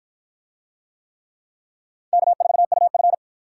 Morse code, heard as a keyed tone, about 50 wpm, 700 Hz. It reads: K4UF